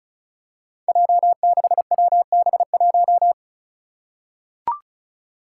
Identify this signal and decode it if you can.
Morse code, heard as a keyed tone, about 36 wpm, 700 Hz. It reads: J6WB1
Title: Five Challenging Letter-Number Combinations 35wpm